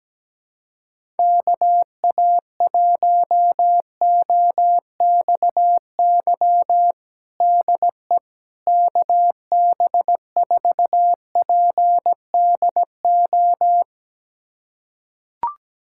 Morse code, heard as a keyed tone, about 17 wpm, 700 Hz. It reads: KA1OXY DE KB4PDO
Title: USA Callsign DE Callsign - 6 Characters - 17wpm